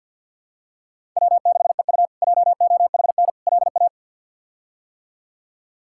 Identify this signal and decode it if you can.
Morse code, heard as a keyed tone, about 50 wpm, 700 Hz. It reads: W6EU JOHN LA
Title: K1USN SST QSO Practice - 50wpm